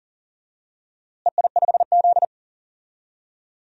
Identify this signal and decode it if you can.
Morse code, heard as a keyed tone, about 40 wpm, 700 Hz. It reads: EI5Z
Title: DXCC-100 Call Signs - 4 Characters - Repeated 3x - 40wpm